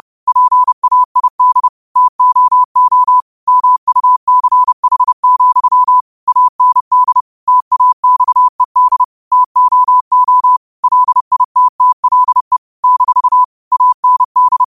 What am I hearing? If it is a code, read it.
Morse code, heard as a keyed tone, about 30 wpm, 1000 Hz. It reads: PAID TOO MUCH, AND TAXED TOO LITTLE - AND